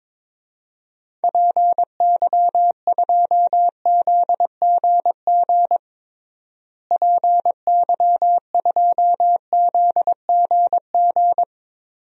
Morse code, heard as a keyed tone, about 22 wpm, 700 Hz. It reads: PY2ZGG PY2ZGG
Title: DXCC-100 Call Signs - 6 Characters - Repeated 2x - 22wpm